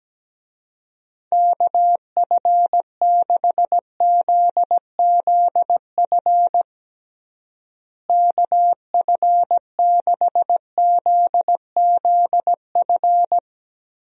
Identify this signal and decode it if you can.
Morse code, heard as a keyed tone, about 17 wpm, 700 Hz. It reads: KF6ZZF KF6ZZF
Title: USA Call Signs - 6 Characters - Repeated 3x - 17wpm